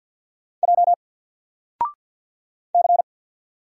Morse code, heard as a keyed tone, about 48 wpm, 700 Hz. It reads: J C
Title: Single Letter-Number Alt 50wpm